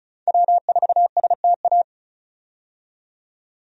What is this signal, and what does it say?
Morse code, heard as a keyed tone, about 34 wpm, 700 Hz. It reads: W4STA
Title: USA Call Signs - 5 Characters - 35wpm